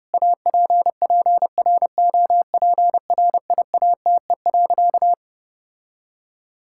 Morse code, heard as a keyed tone, about 30 wpm, 700 Hz. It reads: APPROPRIATE.
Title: Sentences from Top 1000 Words - Review Part 5 - 30wpm